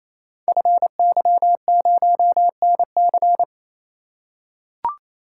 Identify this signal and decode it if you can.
Morse code, heard as a keyed tone, about 29 wpm, 700 Hz. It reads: FY0NC
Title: Code Groups - 5 Letter:Numbers - 28wpm